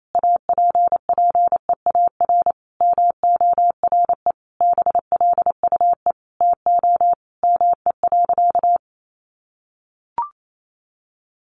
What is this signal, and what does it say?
Morse code, heard as a keyed tone, about 28 wpm, 700 Hz. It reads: APPEAR MORE BLUE TO ME.
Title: Sentences from Top 1200 Words 28wpm